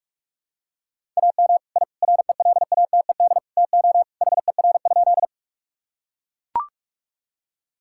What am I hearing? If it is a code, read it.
Morse code, heard as a keyed tone, about 41 wpm, 700 Hz. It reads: AM I RELATED TO HER?
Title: Sentences from Top 1000 Words - Review Part 5 - 45wpm